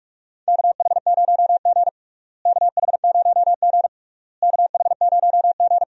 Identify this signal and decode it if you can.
Morse code, heard as a keyed tone, about 46 wpm, 700 Hz. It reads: KH0G KH0G KH0G
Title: USA Call Signs - 4 Characters - Repeated 3x - 45wpm